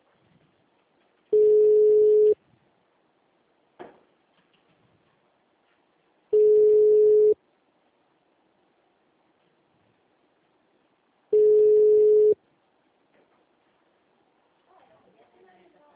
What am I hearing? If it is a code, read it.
Ringback tone (Europe, 425 Hz), with speech.
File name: out- -2002- - - .33016